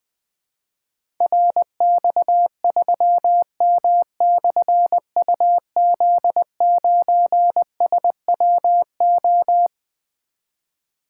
Morse code, heard as a keyed tone, about 20 wpm, 700 Hz. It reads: RX3M/UZ9SWO
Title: Difficult Call Signs 20wpm